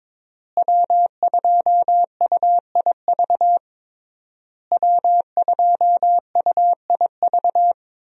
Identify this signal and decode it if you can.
Morse code, heard as a keyed tone, about 21 wpm, 700 Hz. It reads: W2UIV W2UIV
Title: USA Call Signs - 5 Characters - Repeated 3x - 22wpm